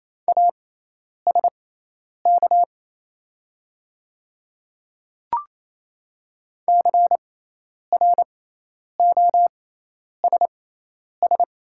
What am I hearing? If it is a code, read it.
Morse code, heard as a keyed tone, about 28 wpm, spaced as for 10 wpm, 700 Hz. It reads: ASK CROSS
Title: 6x Character Spacing - Top 300 Words - 28wpm